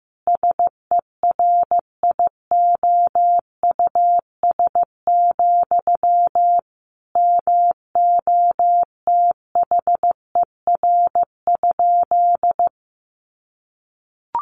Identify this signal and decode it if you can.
Morse code, heard as a keyed tone, about 14 wpm, 700 Hz. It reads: SERIOUS, MOTHER?
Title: Sentences from Top 1100 Words 15wpm